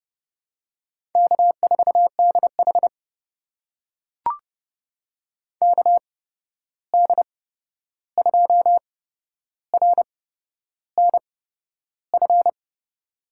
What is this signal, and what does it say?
Morse code, heard as a keyed tone, about 30 wpm, spaced as for 11 wpm, 700 Hz. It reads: K4DH KD2RNF
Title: USA Call Signs - 8x Character Spacing - 30wpm